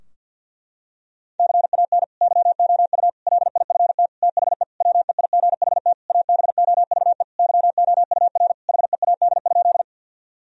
Morse code, heard as a keyed tone, about 50 wpm, 700 Hz. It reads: CAN YOU LIFT THE WEIGHT ABOVE YOUR HEAD?
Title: Sentences from Top 1000 Words - Review Part 3 - 50wpm